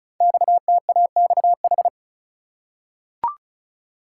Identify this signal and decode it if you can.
Morse code, heard as a keyed tone, about 36 wpm, 700 Hz. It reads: XTAXH
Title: Code Groups - 5 Letters - 35wpm